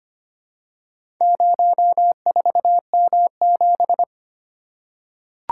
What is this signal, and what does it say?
Morse code, heard as a keyed tone, about 25 wpm, 700 Hz. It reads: 04M7
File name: practice / 25wpm / Four Letter-Number Combinations - 25wpm